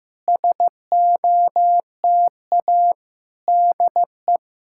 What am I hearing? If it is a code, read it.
Morse code, heard as a keyed tone, about 15 wpm, 700 Hz. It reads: SOTA DE
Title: SOTA QSO Practice 15wpm